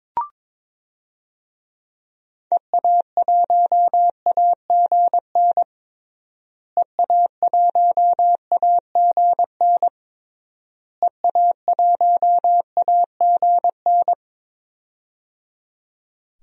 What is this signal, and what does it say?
Morse code, heard as a keyed tone, about 22 wpm, 700 Hz. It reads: EA1AGN EA1AGN EA1AGN
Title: DXCC-100 Call Signs - 6 Characters - Repeated 3x - 22wpm